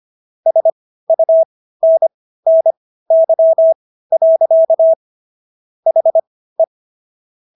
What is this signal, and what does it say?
Morse code, heard as a keyed tone, about 25 wpm, spaced as for 15 wpm, 650 Hz. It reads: SUNNY. HE
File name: SS110 15